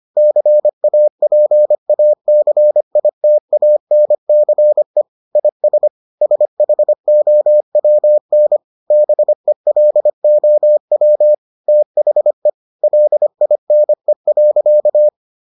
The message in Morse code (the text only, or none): CAPACITANCE IS SHOWN BELOW THE LINE.